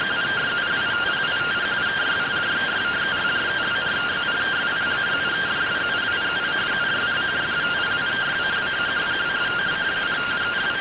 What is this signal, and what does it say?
Two-tone teleprinter signal, about 170 Hz shift.